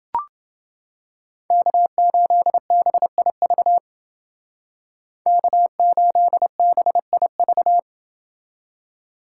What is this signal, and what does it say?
Morse code, heard as a keyed tone, about 29 wpm, 700 Hz. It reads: K8BIV K8BIV